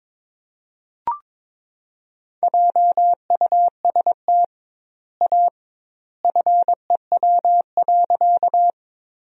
Morse code, heard as a keyed tone, about 22 wpm, 700 Hz. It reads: JUST A FEW.
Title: 2x Word-Spacing - Sentences from Top 300 Words - 22wpm